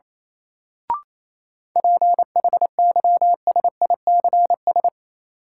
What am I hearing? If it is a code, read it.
Morse code, heard as a keyed tone, about 27 wpm, 700 Hz. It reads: PHYSICS
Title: Sentences from Top 1500 Words - New Word Drill Alt - 28wpm